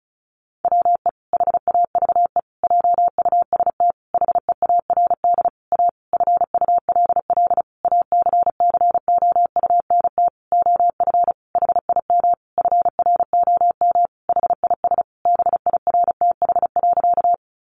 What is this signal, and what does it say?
Morse code, heard as a keyed tone, about 34 wpm, 700 Hz. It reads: WE HAVE JUST HEARD A FULL ACCOUNT OF HIM FROM HIS BIRTH.
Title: Sentences from Top 900 Words 35wpm